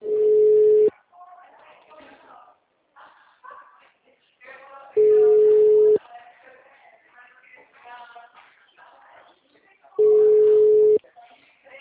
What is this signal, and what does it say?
Ringback tone (Europe, 425 Hz), with speech.